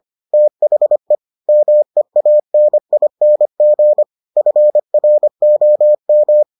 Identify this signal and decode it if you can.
Morse code, heard as a keyed tone, about 25 wpm, 600 Hz. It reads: THE MEANING FROM